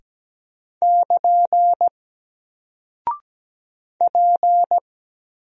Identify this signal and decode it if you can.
Morse code, heard as a keyed tone, about 17 wpm, 700 Hz. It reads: <KN> P
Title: Single Character (Letter:Number:Punctuation:Prosign) 17wpm